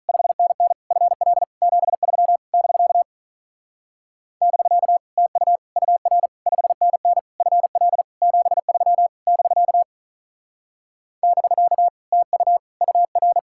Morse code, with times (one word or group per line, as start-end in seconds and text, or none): 0.1-0.7 5NN
0.9-1.4 FL
1.6-2.4 73
2.5-3.0 <BK>
4.4-5.0 <BK>
5.2-5.6 TU
5.8-6.3 UR
6.5-7.2 5NN
7.4-8.0 FL
8.2-9.1 73
9.3-9.8 <BK>
11.2-11.9 <BK>
12.1-12.6 TU
12.8-13.4 UR